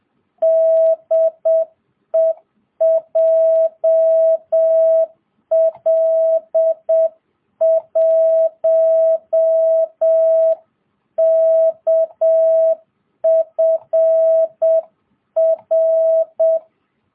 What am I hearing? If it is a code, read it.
Morse code, heard as a keyed tone, about 7 wpm, 660 Hz. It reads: DE JL1KFR